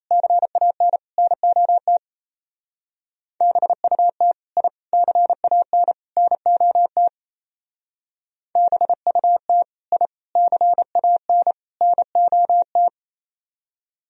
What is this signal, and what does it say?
Morse code, heard as a keyed tone, about 36 wpm, 700 Hz. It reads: CAN NOT BUT I CAN NOT BUT I CAN NOT